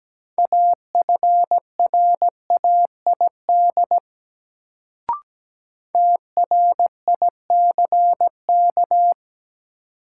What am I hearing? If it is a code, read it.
Morse code, heard as a keyed tone, about 17 wpm, 700 Hz. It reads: AFRAID TRICK
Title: Sentences from Top 1500 Words - New Word Drill - 17wpm